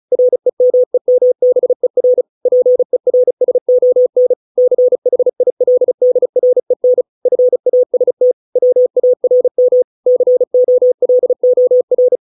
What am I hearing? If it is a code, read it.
Morse code, heard as a keyed tone, about 34 wpm, 500 Hz. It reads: REMEMBER PERSON CHILDREN FAST WARM COLOR